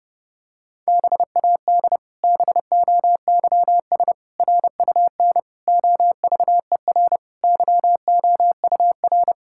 Morse code, heard as a keyed tone, about 30 wpm, 700 Hz. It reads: BAD BOYS RUN OVER YOUR
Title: Pangram Challenge v2 - 30wpm